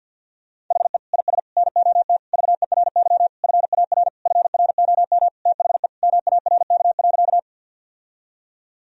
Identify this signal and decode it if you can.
Morse code, heard as a keyed tone, about 51 wpm, 700 Hz. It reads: HE IS NOT VERY FAR FROM THE MARK.
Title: Sentences from Top 1000 Words - Review Part 1 - 50wpm